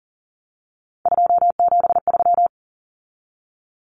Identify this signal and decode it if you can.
Morse code, heard as a keyed tone, about 40 wpm, 700 Hz. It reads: 273
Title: Numbers - 3-Digit 40wpm